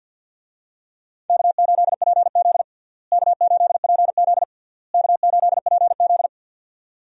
Morse code, heard as a keyed tone, about 52 wpm, 700 Hz. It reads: K8PZ K8PZ K8PZ
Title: USA Call Signs - 4 Characters - Repeated 3x - 50wpm